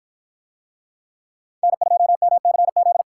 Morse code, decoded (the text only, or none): N2MCZ